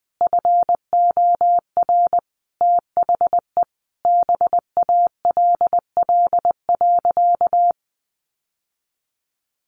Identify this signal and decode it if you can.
Morse code, heard as a keyed tone, about 20 wpm, 700 Hz. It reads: FOR THE BALL.
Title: Sentences from Top 900 Words 20wpm